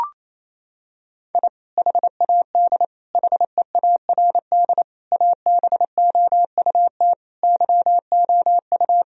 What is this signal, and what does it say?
Morse code, heard as a keyed tone, about 28 wpm, 700 Hz. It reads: I HAD HEARD ABOUT YOU
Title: Sentences from Top 1000 Words - Review Part 1 - 28wpm